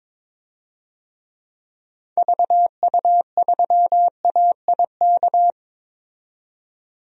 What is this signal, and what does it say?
Morse code, heard as a keyed tone, about 22 wpm, 700 Hz. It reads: VU3AIK
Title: DXCC-100 Call Signs - 6 Characters - 22wpm